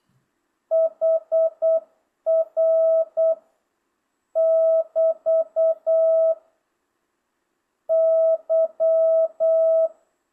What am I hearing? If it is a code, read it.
Morse code, heard as a keyed tone, about 8 wpm, 650 Hz. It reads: HR <BT> Y